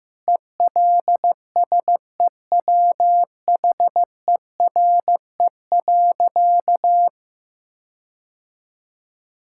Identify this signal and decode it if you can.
Morse code, heard as a keyed tone, about 16 wpm, 700 Hz. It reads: ELSEWHERE.
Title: Sentences from Top 1600 Words 15wpm